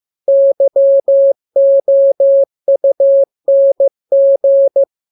Morse code, heard as a keyed tone, about 15 wpm, 550 Hz. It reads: YOUNG
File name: jokes / joke 13 15 0000